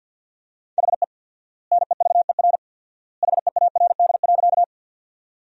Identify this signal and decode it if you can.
Morse code, heard as a keyed tone, about 50 wpm, 700 Hz. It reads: HE NEVER HEARD.
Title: 4x Word-Spacing - Sentences from Top 400 Words - 50wpm